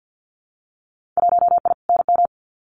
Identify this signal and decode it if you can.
Morse code, heard as a keyed tone, about 50 wpm, 700 Hz. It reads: 1I NM
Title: Winter Field Day Exchanges Encore 50wpm